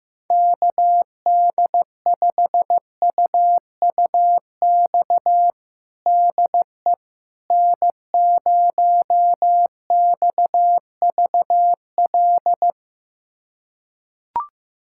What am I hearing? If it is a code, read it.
Morse code, heard as a keyed tone, about 15 wpm, 700 Hz. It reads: KD5UUX DE N0XVL
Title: USA Callsign DE Callsign 15wpm